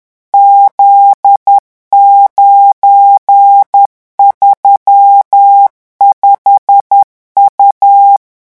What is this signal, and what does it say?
Morse code, heard as a keyed tone, about 10 wpm, 800 Hz. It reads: Z935U
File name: morse mp3 / revision19-vitesse 10 mots